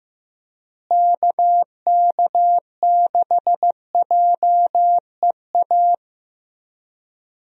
Morse code, heard as a keyed tone, about 15 wpm, 700 Hz. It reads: KK6JEA
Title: USA Call Signs - 6 Characters - Repeated 2x - 15wpm